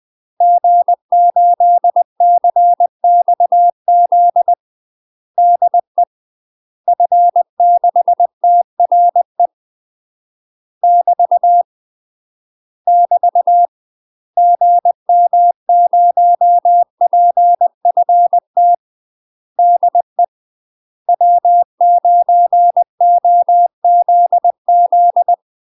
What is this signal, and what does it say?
Morse code, heard as a keyed tone, about 20 wpm, 700 Hz. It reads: G8CXZ DE F6TRE <BT> <BT> GM0PFT DE W9OZZ
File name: UK MASTER QSOs TRAINING / 20WPM / UK MASTER QSOs 0043